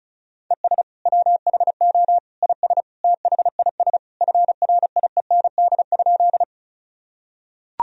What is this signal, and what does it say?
Morse code, heard as a keyed tone, about 36 wpm, 700 Hz. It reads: ES WHO IS THIS FRIEND?
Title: Sentences from Top 300 Words and Abbreviations 35wpm